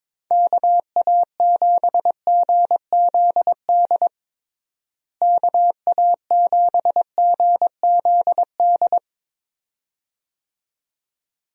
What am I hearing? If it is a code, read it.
Morse code, heard as a keyed tone, about 22 wpm, 700 Hz. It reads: KA7GZD KA7GZD
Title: USA Call Signs - 6 Characters - Repeated 2x - 22wpm